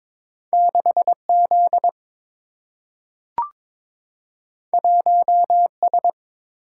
Morse code, heard as a keyed tone, about 22 wpm, 700 Hz. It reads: 6 Z 1 S
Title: Single-Letter Single-Number 22wpm